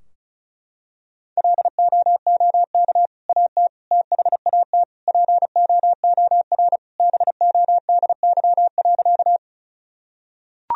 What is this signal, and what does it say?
Morse code, heard as a keyed tone, about 36 wpm, 700 Hz. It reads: LOOK AT THAT POOR BODY.
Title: Sentences from Top 1000 Words - Review Part 2 - 35wpm